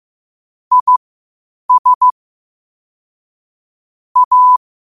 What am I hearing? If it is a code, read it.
Morse code, heard as a keyed tone, about 15 wpm, spaced as for 8 wpm, 1000 Hz. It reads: IS A
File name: qotdfiles / 8wpm